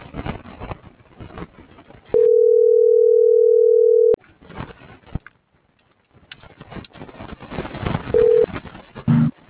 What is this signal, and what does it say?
Ringback tone (North America), 440 + 480 Hz.